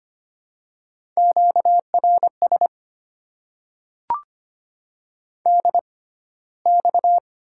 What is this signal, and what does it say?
Morse code, heard as a keyed tone, about 25 wpm, spaced as for 15 wpm, 690 Hz. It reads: QRS DX
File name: Top 90 QSO Elements - 6x Character Spacing - 25wpm